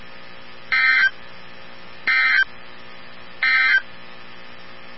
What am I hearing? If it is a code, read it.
Emergency Alert System (SAME) tones, heard as harsh buzzing bursts. It is heard from 0.6 to 3.9 s.
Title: NOAA Weather Radio 162.550